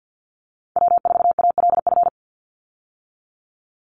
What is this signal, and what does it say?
Morse code, heard as a keyed tone, about 50 wpm, 700 Hz. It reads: W4ALF
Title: Call Signs - 50wpm